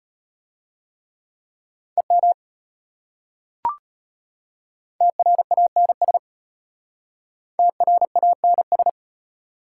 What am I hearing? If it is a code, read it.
Morse code, heard as a keyed tone, about 31 wpm, 700 Hz. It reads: EM TRANS TRANS